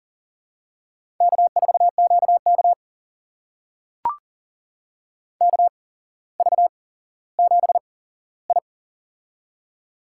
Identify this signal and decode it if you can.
Morse code, heard as a keyed tone, about 40 wpm, spaced as for 17 wpm, 700 Hz. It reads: K4QK KV7I
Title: USA Call Signs - 4 Characters - 8x Character Spacing - 40wpm